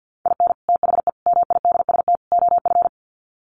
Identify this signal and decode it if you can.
Morse code, heard as a keyed tone, about 50 wpm, 700 Hz. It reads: IN THE MIDST OF
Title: Sets of 4 Words 50wpm